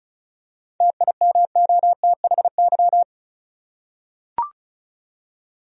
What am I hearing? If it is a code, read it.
Morse code, heard as a keyed tone, about 34 wpm, 700 Hz. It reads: TIMOTHY
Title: USA Common Names - 35wpm